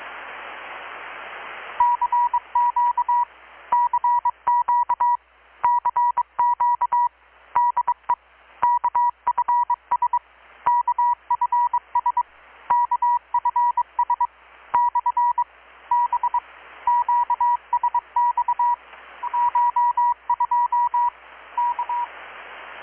Morse code, heard as a keyed tone, about 22 wpm, 970 Hz. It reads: CQ CQ CQ DE KFS KFS KFS / B QSX CH2 K
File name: blog / swl / 6-abril-2019-12695-USB-KPS